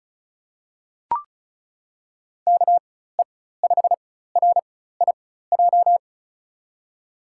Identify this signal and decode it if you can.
Morse code, heard as a keyed tone, about 36 wpm, spaced as for 16 wpm, 700 Hz. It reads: KE5RIJ
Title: USA Call Signs - 4x Character Spacing - 35wpm